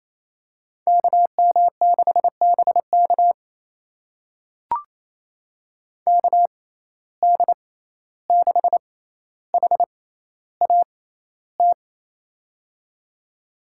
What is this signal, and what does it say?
Morse code, heard as a keyed tone, about 28 wpm, spaced as for 14 wpm, 700 Hz. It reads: KM6BK KD6HAT
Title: USA Call Signs - 6x Character Spacing - 28wpm